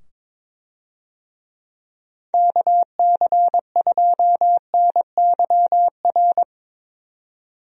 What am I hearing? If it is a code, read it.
Morse code, heard as a keyed tone, about 21 wpm, 700 Hz. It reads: KC2NYR
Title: USA Call Signs - 6 Characters - 22wpm